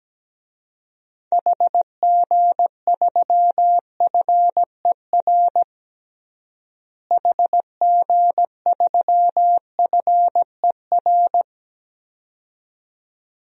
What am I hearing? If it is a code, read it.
Morse code, heard as a keyed tone, about 17 wpm, 700 Hz. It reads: HG3FER HG3FER
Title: DXCC-100 Call Signs - 6 Characters - Repeated 2x - 17wpm